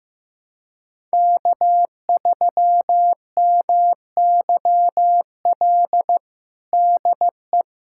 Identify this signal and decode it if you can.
Morse code, heard as a keyed tone, about 15 wpm, 700 Hz. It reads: K3MYL DE
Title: USA Callsign DE Callsign - 5 Characters - 15wpm